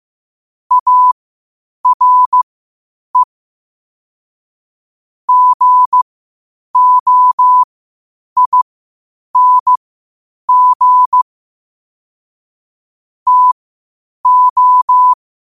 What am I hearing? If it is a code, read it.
Morse code, heard as a keyed tone, about 15 wpm, spaced as for 8 wpm, 1000 Hz. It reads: ARE GOING TO
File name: qotdfiles / 8wpm